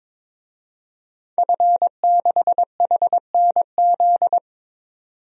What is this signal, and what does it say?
Morse code, heard as a keyed tone, about 22 wpm, 700 Hz. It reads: F6HNZ
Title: DXCC-100 Call Signs - 5 Characters - 22wpm